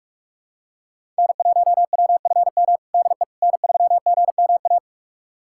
Morse code, heard as a keyed tone, about 46 wpm, 700 Hz. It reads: N1WFM DE N3GMA
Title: USA Callsign DE Callsign - 5 Characters - 45wpm